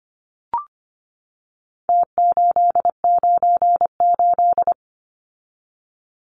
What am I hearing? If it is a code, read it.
Morse code, heard as a keyed tone, about 25 wpm, 700 Hz. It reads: T898
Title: Numbers - Contest Style - 4-Digit 25wpm